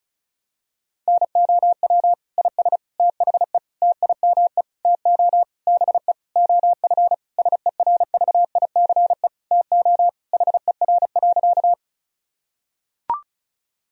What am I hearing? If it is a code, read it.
Morse code, heard as a keyed tone, about 36 wpm, 700 Hz. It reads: NOW IS THE TIME TO BE OF SERVICE TO HER.
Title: Sentences from Top 1000 Words - Review Part 4 - 35wpm